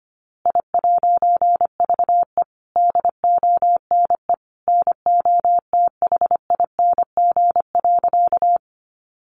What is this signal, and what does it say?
Morse code, heard as a keyed tone, about 25 wpm, 700 Hz. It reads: I'VE DONE NOTHING.
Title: A Sound Of Thunder 25wpm